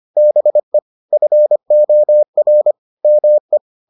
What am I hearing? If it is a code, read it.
Morse code, heard as a keyed tone, about 25 wpm, 600 Hz. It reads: BE FOR ME